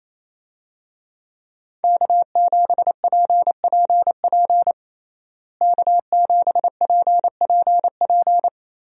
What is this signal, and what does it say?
Morse code, heard as a keyed tone, about 28 wpm, 700 Hz. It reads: K7PPP K7PPP
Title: USA Call Signs - 5 Characters - Repeated 2x - 28wpm